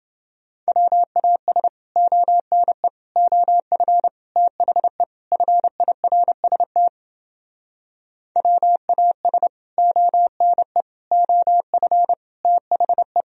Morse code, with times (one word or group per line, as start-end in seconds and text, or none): 0.7-1.7 WAS
2.0-2.9 ONE
3.2-4.1 OF
4.4-5.0 THE
5.3-6.9 FIRST
8.4-9.5 WAS
9.8-10.8 ONE
11.1-12.1 OF
12.4-13.2 THE